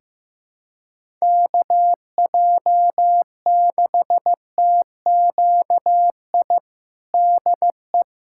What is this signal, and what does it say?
Morse code, heard as a keyed tone, about 15 wpm, 700 Hz. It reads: KJ6TQI DE
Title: USA Callsign DE Callsign 15wpm